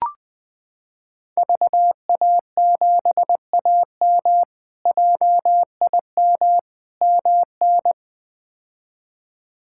Morse code, heard as a keyed tone, about 20 wpm, 700 Hz. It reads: VA7AM JIM MN
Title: K1USN SST QSO Practice - 20wpm